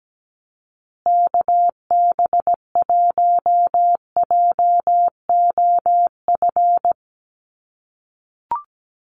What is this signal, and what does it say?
Morse code, heard as a keyed tone, about 17 wpm, 700 Hz. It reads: KB1JOF
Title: USA Call Signs Repeated 2x 17wpm